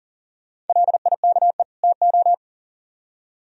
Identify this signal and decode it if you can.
Morse code, heard as a keyed tone, about 51 wpm, 700 Hz. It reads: LIKE TO